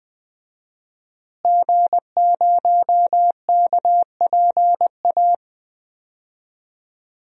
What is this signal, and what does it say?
Morse code, heard as a keyed tone, about 21 wpm, 700 Hz. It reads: G0KPA
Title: DXCC-100 Call Signs - 5 Characters - Repeated 2x - 20wpm